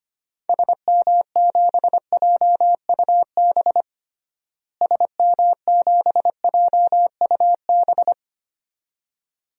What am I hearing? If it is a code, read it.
Morse code, heard as a keyed tone, about 25 wpm, 700 Hz. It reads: SM7JUB SM7JUB
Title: DXCC-100 Call Signs - 6 Characters - Repeated 2x - 25wpm